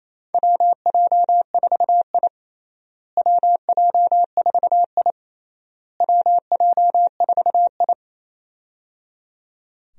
Morse code, heard as a keyed tone, about 28 wpm, 700 Hz. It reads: WJ4I WJ4I WJ4I
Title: USA Call Signs - 4 Characters - Repeated 3x - 28wpm